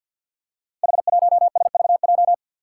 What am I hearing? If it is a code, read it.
Morse code, heard as a keyed tone, about 53 wpm, 700 Hz. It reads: H1SVJ
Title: Five Challenging Letter-Number Combinations 50wpm